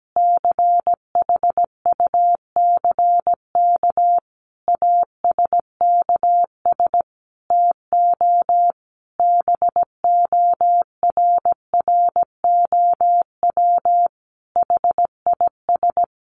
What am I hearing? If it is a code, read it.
Morse code, heard as a keyed tone, about 17 wpm, 700 Hz. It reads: CHUCK ASKS TO BORROW HIS
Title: Additional Easy Phrases 17wpm